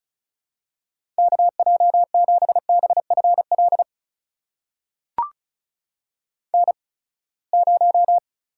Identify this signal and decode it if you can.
Morse code, heard as a keyed tone, about 36 wpm, spaced as for 24 wpm, 700 Hz. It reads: KJ7BFL N0
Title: USA Call Signs - 8x Character Spacing - 35wpm